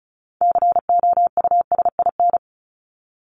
Morse code, heard as a keyed tone, about 34 wpm, 700 Hz. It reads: COUSIN
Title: Top 1501-1750 Words 35wpm